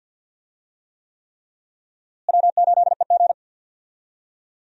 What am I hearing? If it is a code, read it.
Morse code, heard as a keyed tone, about 53 wpm, 700 Hz. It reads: W8EG
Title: USA Call Signs - 4 Characters - Repeated 3x - 50wpm